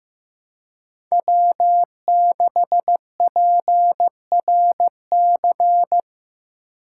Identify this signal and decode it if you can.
Morse code, heard as a keyed tone, about 16 wpm, 700 Hz. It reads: W6PRC
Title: USA Call Signs - 5 Characters - 3x Character Spacing - 15wpm